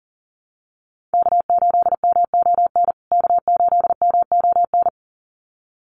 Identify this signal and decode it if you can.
Morse code, heard as a keyed tone, about 37 wpm, 700 Hz. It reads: K8MON K8MON
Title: USA Call Signs Repeated 2x 40wpm